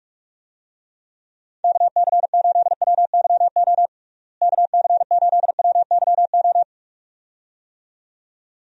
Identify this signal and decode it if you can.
Morse code, heard as a keyed tone, about 45 wpm, 700 Hz. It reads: KC8WYO KC8WYO
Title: USA Call Signs - 6 Characters - Repeated 2x - 45wpm